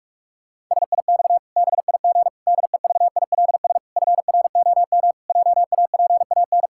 Morse code, heard as a keyed tone, about 41 wpm, 700 Hz. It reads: SIX BIG DEVILS FROM JAPAN
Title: Pangram Challenge v2 - 45wpm